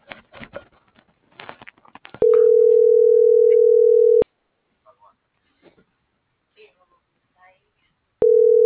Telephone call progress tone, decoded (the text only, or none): ringback tone